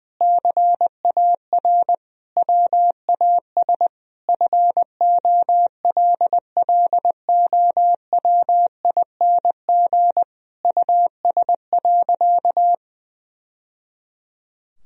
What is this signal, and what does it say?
Morse code, heard as a keyed tone, about 21 wpm, 700 Hz. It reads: CAR WAS FOLLOWING US.
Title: Sentences from Top 1000 Words - Review Part 4 - 20wpm